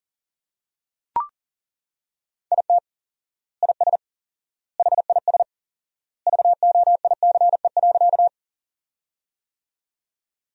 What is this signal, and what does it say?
Morse code, heard as a keyed tone, about 37 wpm, 700 Hz. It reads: IT IS HIS VOICE.
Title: 4x Word-Spacing - Sentences from Top 500 Words - 40wpm